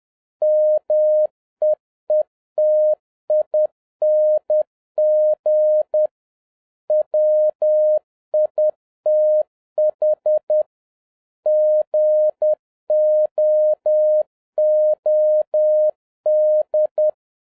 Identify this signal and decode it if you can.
Morse code, heard as a keyed tone, about 10 wpm, 610 Hz. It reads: MEETING WITH GOOD
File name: 2011-10-mp3 / n11-MEETING WITH GOOD FRIENDS 10wpm